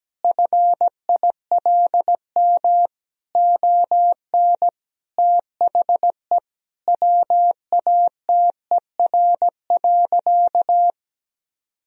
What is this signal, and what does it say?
Morse code, heard as a keyed tone, about 17 wpm, 700 Hz. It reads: FILM ON THE WATER.
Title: Sentences from Top 1000 Words - Review Part 3 - 17wpm